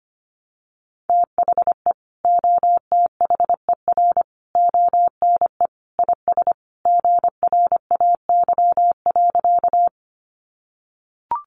Morse code, heard as a keyed tone, about 25 wpm, 700 Hz. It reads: THE OTHER ONE IS GRAY.
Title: Additional Easy Phrases 25wpm